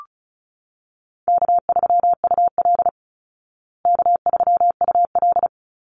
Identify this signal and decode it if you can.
Morse code, heard as a keyed tone, about 34 wpm, 700 Hz. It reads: K3UL K3UL
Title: Easy Call Signs Repeated 3x - 35wpm